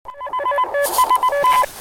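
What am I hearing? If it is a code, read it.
Morse code, heard as a keyed tone, about 19 wpm, 1010 Hz, with noise. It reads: S ST
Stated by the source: Morse QSO between QRP stations during a winter field day QRP event, in light snow.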